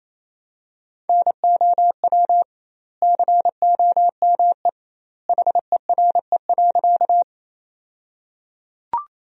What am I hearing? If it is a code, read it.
Morse code, heard as a keyed tone, about 27 wpm, 700 Hz. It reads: NOW COME HERE.
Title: 2x Word-Spacing - Sentences from Top 200 Words - 28wpm